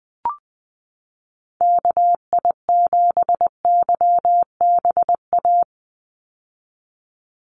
Morse code, heard as a keyed tone, about 20 wpm, 700 Hz. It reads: KI7YBA
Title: USA Call Signs 20wpm